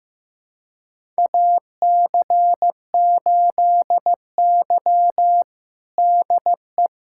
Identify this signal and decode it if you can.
Morse code, heard as a keyed tone, about 15 wpm, 700 Hz. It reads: AC8Y DE
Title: USA Callsign DE Callsign - 4 Characters - 15wpm